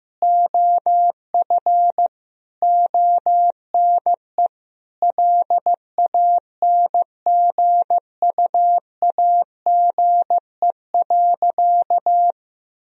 Morse code, heard as a keyed tone, about 15 wpm, 700 Hz. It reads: OF ONE LANGUAGE.
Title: Sentences from Top 1000 Words - Review Part 3 - 15wpm